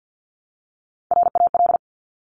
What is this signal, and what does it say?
Morse code, heard as a keyed tone, about 50 wpm, 700 Hz. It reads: RAL
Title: Top Three Letter Combinations in Words 50wpm